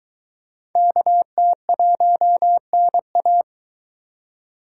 Morse code, heard as a keyed tone, about 21 wpm, 700 Hz. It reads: KT1NA